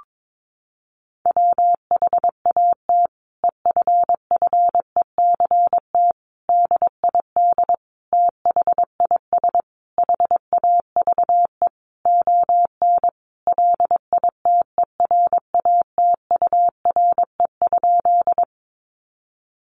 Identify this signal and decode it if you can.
Morse code, heard as a keyed tone, about 22 wpm, 700 Hz. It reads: WHAT EFFECT DID THIS HAVE ON LITERATURE?
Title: Sentences from Top 1000 Words 22wpm